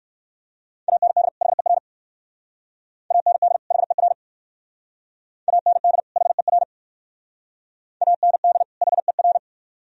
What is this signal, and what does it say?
Morse code, heard as a keyed tone, about 59 wpm, 700 Hz. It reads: AND HER AND HER AND HER AND HER